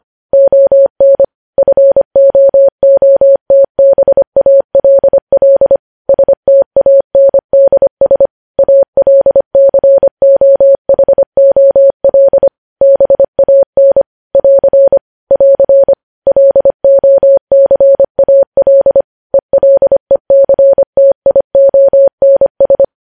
Morse code, heard as a keyed tone, about 25 wpm, 550 Hz. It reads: ON FOOTBALL STANDS ALCOHOL BAN <AR> <AR> LOCAL ELECTIONS
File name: podcast / News headlines-1x-25x25